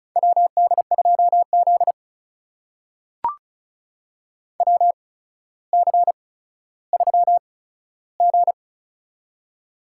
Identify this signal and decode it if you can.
Morse code, heard as a keyed tone, about 36 wpm, spaced as for 15 wpm, 700 Hz. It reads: WD2Z WC3G
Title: USA Call Signs - 4 Characters - 8x Character Spacing - 35wpm